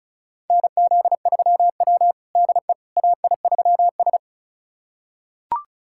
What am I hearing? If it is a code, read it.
Morse code, heard as a keyed tone, about 36 wpm, 700 Hz. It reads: NZ3W DE AI3S
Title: USA Callsign DE Callsign - 4 Characters - 35wpm